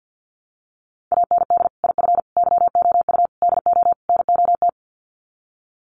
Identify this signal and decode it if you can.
Morse code, heard as a keyed tone, about 50 wpm, 700 Hz. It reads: AND IF YOU DO NOT
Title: Sets of 5 Words 50wpm